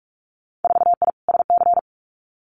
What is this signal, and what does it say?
Morse code, heard as a keyed tone, about 46 wpm, 700 Hz. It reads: 4I SC
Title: Winter Field Day Exchanges 45wpm